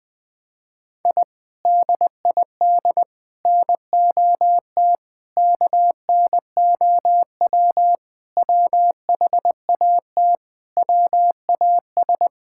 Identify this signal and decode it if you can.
Morse code, heard as a keyed tone, about 21 wpm, 700 Hz. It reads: I DID NOT KNOW WHAT WAS
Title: Sentences with 4-Word Phrases - 20wpm